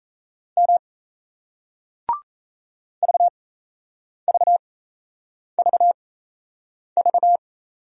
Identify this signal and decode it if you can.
Morse code, heard as a keyed tone, about 38 wpm, spaced as for 8 wpm, 700 Hz. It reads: M VVVV